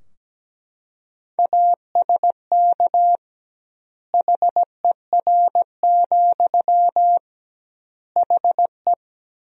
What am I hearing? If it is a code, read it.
Morse code, heard as a keyed tone, about 17 wpm, 700 Hz. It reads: ASK HER, HE
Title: 2x Word-Spacing - Sentences from Top 200 Words - 17wpm